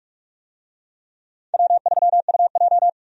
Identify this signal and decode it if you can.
Morse code, heard as a keyed tone, about 46 wpm, 700 Hz. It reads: W3UJ
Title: USA Call Signs - 4 Characters - 45wpm